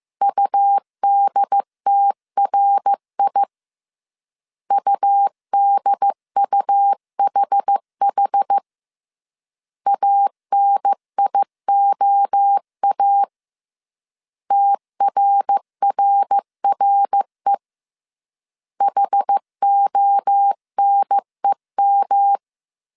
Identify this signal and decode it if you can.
Morse code, heard as a keyed tone, about 15 wpm, 800 Hz. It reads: UDTRI UDUHH ANIOA TRRRE HONEM
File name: CW / DET / 1 3 1500C S A 14wpm 2447